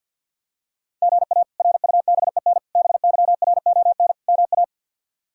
Morse code, heard as a keyed tone, about 46 wpm, 700 Hz. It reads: GA RUBEN BYRON MA
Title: K1USN SST QSO Practice - 50wpm